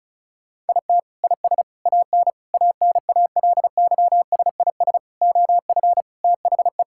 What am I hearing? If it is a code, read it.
Morse code, heard as a keyed tone, about 34 wpm, 700 Hz. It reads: IT IS AN ANALYSIS OF THE
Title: Sentences from Top 1000 Words - Review Part 5 - 35wpm